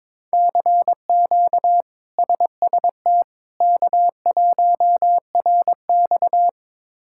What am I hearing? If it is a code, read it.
Morse code, heard as a keyed tone, about 22 wpm, 700 Hz. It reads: CQ SST K1RX
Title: K1USN SST QSO Practice - 22wpm